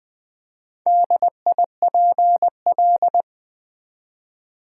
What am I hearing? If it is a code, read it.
Morse code, heard as a keyed tone, about 20 wpm, 700 Hz. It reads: DIPL
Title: Build A Word - 2 Letters At A Time - 8-Letter Words - 20wpm